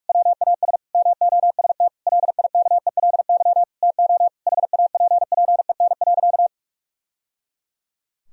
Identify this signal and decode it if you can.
Morse code, heard as a keyed tone, about 42 wpm, 700 Hz. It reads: WAS MOST LIKELY TO HAPPEN.
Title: Sentences from Top 1000 Words - Review Part 4 - 45wpm